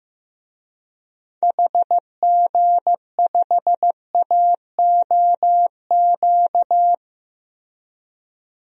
Morse code, heard as a keyed tone, about 15 wpm, 700 Hz. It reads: HG5AOQ
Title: DXCC-100 Call Signs - 6 Characters - Repeated 3x - 15wpm